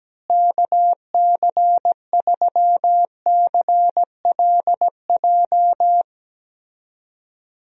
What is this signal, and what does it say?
Morse code, heard as a keyed tone, about 17 wpm, 700 Hz. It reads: KC3CLJ
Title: USA Call Signs - 6 Characters - Repeated 3x - 17wpm